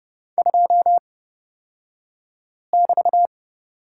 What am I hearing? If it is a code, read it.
Morse code, heard as a keyed tone, about 30 wpm, spaced as for 5 wpm, 690 Hz. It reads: 2<BT>
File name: Lesson 36d - BT - Two Characters 30wpm